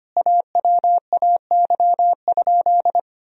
Morse code, heard as a keyed tone, about 25 wpm, 700 Hz. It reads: AWAY?
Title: Sentences from Top 1000 Words - Review Part 2 - 25wpm